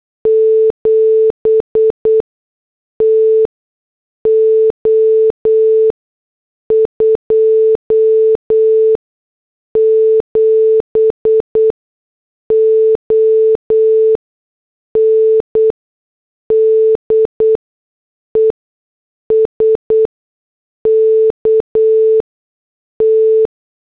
Morse code, heard as a keyed tone, about 8 wpm, spaced as for 6 wpm, 440 Hz. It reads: 7 TO 27 ON DESKT
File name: morse / headlines